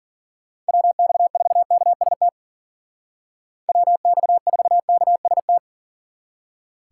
Morse code, heard as a keyed tone, about 50 wpm, 700 Hz. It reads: WX4KST WX4KST
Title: Speed Racing - USA Call Signs 40wpm